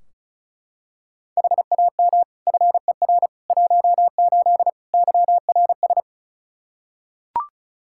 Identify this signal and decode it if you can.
Morse code, heard as a keyed tone, about 36 wpm, 700 Hz. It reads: HAM FER 18 YRS
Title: Practice QSO - Exchange 3 - More Info - 35wpm